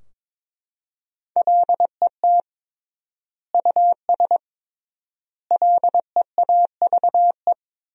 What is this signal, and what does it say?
Morse code, heard as a keyed tone, about 22 wpm, 700 Hz. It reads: LET US LEAVE
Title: 3x Word-Spacing - Sentences from Top 400 Words - 22wpm